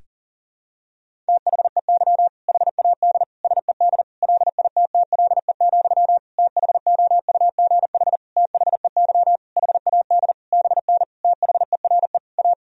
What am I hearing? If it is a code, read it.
Morse code, heard as a keyed tone, about 40 wpm, 700 Hz. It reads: THEY HAD SED LITTLE, THOUGH THEY HAD BN THERE A
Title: Sentences from Top 400 Words and Abbreviations 40wpm